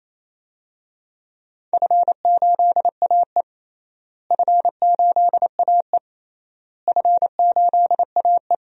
Morse code, heard as a keyed tone, about 28 wpm, 700 Hz. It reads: F8AE F8AE F8AE
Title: DXCC-100 Call Signs - 4 Characters - Repeated 3x - 28wpm